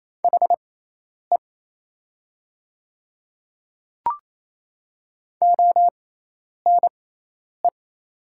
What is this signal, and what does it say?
Morse code, heard as a keyed tone, about 28 wpm, spaced as for 10 wpm, 700 Hz. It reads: HE ONE
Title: 6x Character Spacing - Top 100 Words - 28wpm